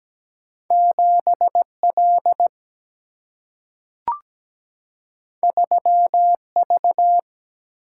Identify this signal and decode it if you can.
Morse code, heard as a keyed tone, about 17 wpm, 700 Hz. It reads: 7L 3V
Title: Single-Letter Single-Number 17wpm